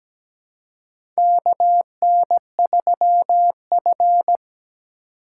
Morse code, heard as a keyed tone, about 17 wpm, 700 Hz. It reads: KN3F
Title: USA Call Signs - 2x Character Spacing - 17wpm